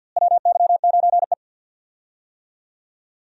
Morse code, heard as a keyed tone, about 53 wpm, 700 Hz. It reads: WY9E
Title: USA Call Signs - 4 Characters - Repeated 2x - 50wpm